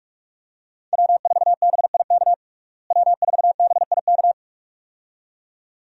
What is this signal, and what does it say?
Morse code, heard as a keyed tone, about 47 wpm, 700 Hz. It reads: W4BIK W4BIK
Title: USA Call Signs - 5 Characters - Repeated 2x - 45wpm